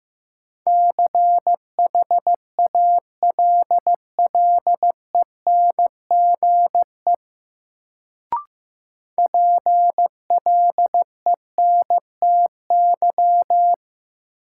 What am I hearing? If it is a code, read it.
Morse code, heard as a keyed tone, about 15 wpm, 700 Hz. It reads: CHALLENGE PLENTY
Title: Sentences from Top 1300 Words - New Word Drill Alt - 15wpm